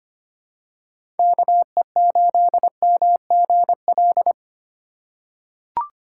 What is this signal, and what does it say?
Morse code, heard as a keyed tone, about 25 wpm, 700 Hz. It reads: KE8MGL
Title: USA Call Signs - 6 Characters - 4x Character Spacing - 25wpm